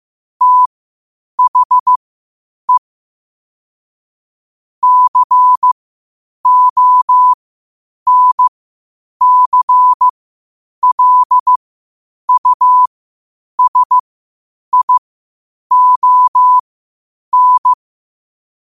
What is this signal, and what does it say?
Morse code, heard as a keyed tone, about 15 wpm, spaced as for 8 wpm, 1000 Hz. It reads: THE CONCLUSION
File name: qotdfiles / 8wpm